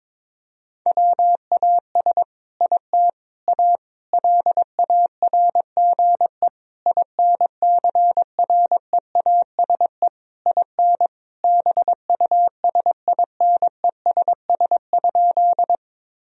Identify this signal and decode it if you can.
Morse code, heard as a keyed tone, about 22 wpm, 700 Hz. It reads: WAS IT A LARGE INCREASE IN BUSINESS?
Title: Sentences from Top 1000 Words - Review Part 4 - 22wpm